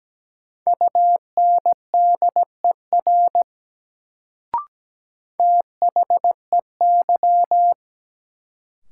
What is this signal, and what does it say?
Morse code, heard as a keyed tone, about 17 wpm, 700 Hz. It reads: UNDER THEY
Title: Sets of 2 Words - New Word Drill - 17wpm